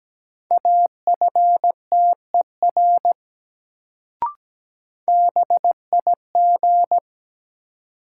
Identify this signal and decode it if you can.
Morse code, heard as a keyed tone, about 17 wpm, 700 Hz. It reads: AFTER BIG
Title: Sets of 2 Words - New Word Drill - 17wpm